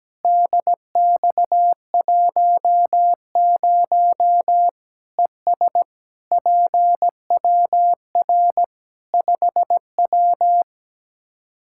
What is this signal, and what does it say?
Morse code, heard as a keyed tone, about 17 wpm, 700 Hz. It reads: DX10 ES PWR 5W
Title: Practice QSO - Exchange 2 - Info - 17wpm